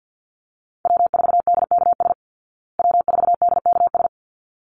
Morse code, heard as a keyed tone, about 49 wpm, 700 Hz. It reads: W4DKS W4DKS
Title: Call Signs Repeated 3x - 50wpm